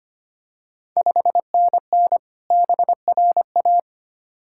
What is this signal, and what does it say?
Morse code, heard as a keyed tone, about 25 wpm, 700 Hz. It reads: 5NN BRA
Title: Ontario QSO Party Exchange 25wpm